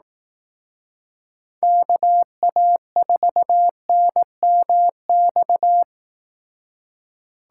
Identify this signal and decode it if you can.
Morse code, heard as a keyed tone, about 18 wpm, 700 Hz. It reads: KA4NMX
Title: Speed Racing - USA Call Signs 15wpm